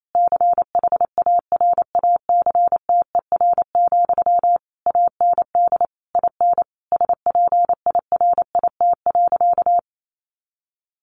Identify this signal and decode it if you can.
Morse code, heard as a keyed tone, about 28 wpm, 700 Hz. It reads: CHARACTER, AND IN SPIRIT.
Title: Sentences from Top 1400 Words 28wpm